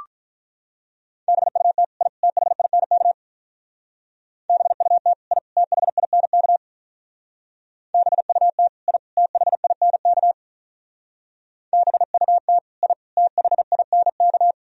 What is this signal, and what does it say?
Morse code, heard as a keyed tone, about 43 wpm, 700 Hz. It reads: BUT I THINK BUT I THINK BUT I THINK BUT I THINK
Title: Speed-Racing - Sets of 3 Words 35wpm